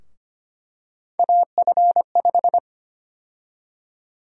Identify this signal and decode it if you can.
Morse code, heard as a keyed tone, about 25 wpm, 700 Hz. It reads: AF5
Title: USA Call Sign Prefixes - Repeated 2x - 25wpm